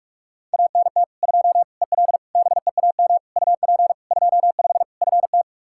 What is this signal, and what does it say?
Morse code, heard as a keyed tone, about 44 wpm, 700 Hz. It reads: ANT 2 EL BEAM UP 25 FT
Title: Practice QSO - Exchange 2 - Info - 45wpm